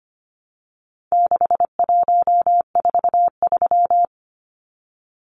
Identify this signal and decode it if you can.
Morse code, heard as a keyed tone, about 25 wpm, 700 Hz. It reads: 6143
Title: Numbers - 4-Digit 25wpm